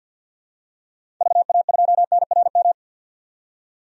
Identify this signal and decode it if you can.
Morse code, heard as a keyed tone, about 46 wpm, 700 Hz. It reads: VA2NRM
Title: DXCC-100 Call Signs - 6 Characters - 50wpm